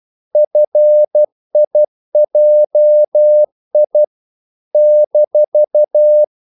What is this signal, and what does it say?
Morse code, heard as a keyed tone, about 12 wpm, 600 Hz. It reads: FIJI -
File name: CW / Practice / NationalCapitals12